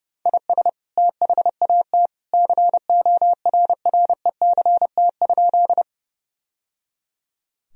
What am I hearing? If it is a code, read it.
Morse code, heard as a keyed tone, about 30 wpm, 700 Hz. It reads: IS THAT CORRECT?
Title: Sentences from Top 1000 Words - Review Part 2 - 30wpm